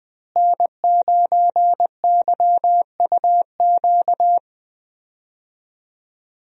Morse code, heard as a keyed tone, about 20 wpm, 700 Hz. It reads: N9YUQ
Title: USA Call Signs - 5 Characters - Repeated 2x - 20wpm